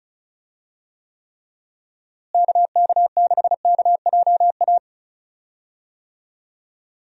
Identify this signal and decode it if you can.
Morse code, heard as a keyed tone, about 36 wpm, 700 Hz. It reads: KK6KJA
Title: USA Call Signs - 6 Characters - Repeated 3x - 35wpm